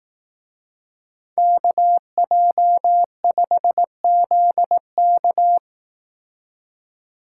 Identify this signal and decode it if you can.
Morse code, heard as a keyed tone, about 18 wpm, 700 Hz. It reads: KJ5ZK
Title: Speed Racing - USA 2x2 Call Signs 15wpm